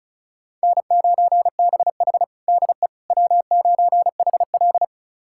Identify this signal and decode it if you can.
Morse code, heard as a keyed tone, about 36 wpm, 700 Hz. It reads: N9BH DE W9HL
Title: USA Callsign DE Callsign - 4 Characters - 35wpm